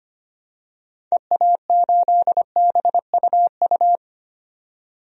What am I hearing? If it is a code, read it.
Morse code, heard as a keyed tone, about 25 wpm, 700 Hz. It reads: EA8BUU
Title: DXCC-100 Call Signs - 6 Characters - 25wpm